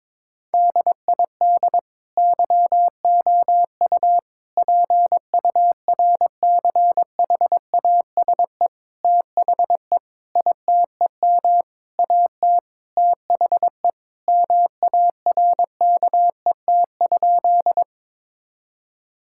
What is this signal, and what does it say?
Morse code, heard as a keyed tone, about 22 wpm, 700 Hz. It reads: DID YOU PURCHASE THE ITEM AT THE MARKET?
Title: Sentences from Top 1000 Words - Review Part 6 - 22wpm